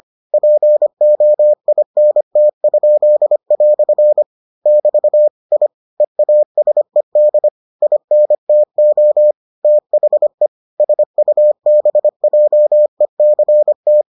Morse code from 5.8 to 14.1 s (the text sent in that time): EASED INTO THE SUBJECT